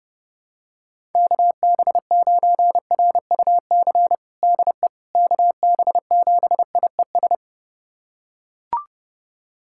Morse code, heard as a keyed tone, about 28 wpm, 700 Hz. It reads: KB9RUC DE KB7IES
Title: USA Callsign DE Callsign - 6 Characters - 30wpm